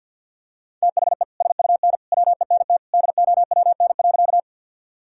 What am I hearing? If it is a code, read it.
Morse code, heard as a keyed tone, about 52 wpm, 700 Hz. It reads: THE SUN WENT DOWN.
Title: Sentences from Top 1000 Words - Review Part 1 - 50wpm